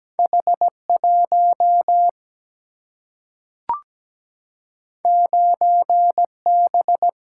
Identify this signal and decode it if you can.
Morse code, heard as a keyed tone, about 17 wpm, 700 Hz. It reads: H1 9B
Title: Single-Letter Single-Number 17wpm